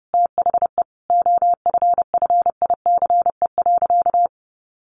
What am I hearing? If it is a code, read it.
Morse code, heard as a keyed tone, about 30 wpm, 700 Hz. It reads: THE OFFICE.
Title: Sentences from Top 1500 Words 30wpm